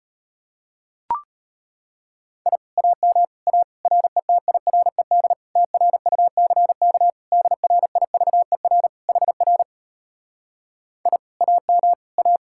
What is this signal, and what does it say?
Morse code, heard as a keyed tone, about 33 wpm, 700 Hz. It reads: I AM A RETIRED TRUCK DRIVER HR I AM A